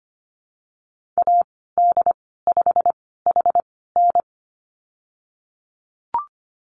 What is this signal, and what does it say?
Morse code, heard as a keyed tone, about 25 wpm, spaced as for 16 wpm, 700 Hz. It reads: AD5HN
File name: Lesson 15g - Number 5 - Call Signs 25wpm